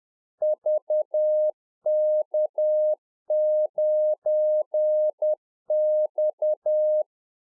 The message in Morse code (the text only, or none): VK9X